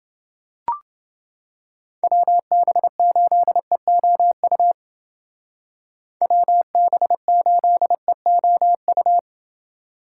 Morse code, heard as a keyed tone, about 28 wpm, 700 Hz. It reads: WB8EOU WB8EOU